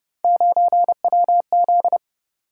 Morse code, heard as a keyed tone, about 30 wpm, 700 Hz. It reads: 9WZ
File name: Speed-Racing - Three Letter-Number Combinations - 25wpm